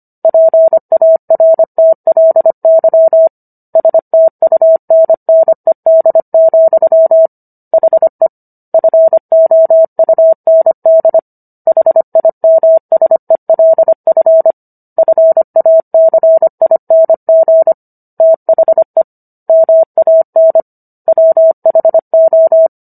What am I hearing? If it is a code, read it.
Morse code, heard as a keyed tone, about 25 wpm, 650 Hz. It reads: PARTLY STUNNED, HE FOUND HIMSELF FACING THE MAN WHO